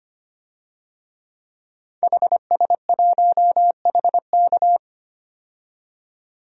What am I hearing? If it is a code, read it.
Morse code, heard as a keyed tone, about 25 wpm, 700 Hz. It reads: HS1HK
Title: DXCC-100 Call Signs - 5 Characters - 25wpm